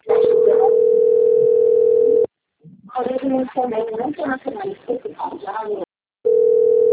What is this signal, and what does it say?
Ringback tone (North America), 440 + 480 Hz, with speech.
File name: out- - - - - .70023